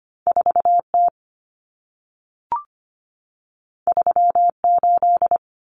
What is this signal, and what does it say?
Morse code, heard as a keyed tone, about 25 wpm, 700 Hz. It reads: 4T 38
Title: Common Cut Numbers - 2-Digit 25wpm